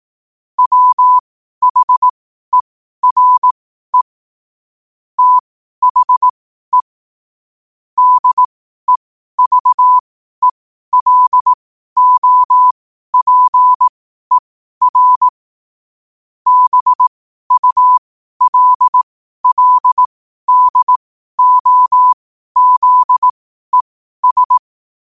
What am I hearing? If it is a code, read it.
Morse code, heard as a keyed tone, about 18 wpm, spaced as for 12 wpm, 1000 Hz. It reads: WHERE THE DEVELOPER BULLDOZES
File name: 12wpm